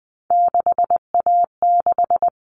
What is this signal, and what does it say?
Morse code, heard as a keyed tone, about 20 wpm, 700 Hz. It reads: 6A6
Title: All Cut Numbers - 3-Digit 20wpm